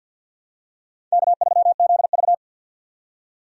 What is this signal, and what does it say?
Morse code, heard as a keyed tone, about 53 wpm, 700 Hz. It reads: K3ZV
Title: USA Call Signs - 4 Characters - 50wpm